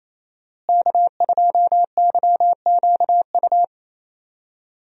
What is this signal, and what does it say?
Morse code, heard as a keyed tone, about 27 wpm, 700 Hz. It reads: K2YQU
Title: USA Call Signs - 5 Characters - Repeated 3x - 28wpm